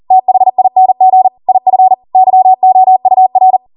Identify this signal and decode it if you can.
Morse code, heard as a keyed tone, about 40 wpm, 750 Hz. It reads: THING IF YOUR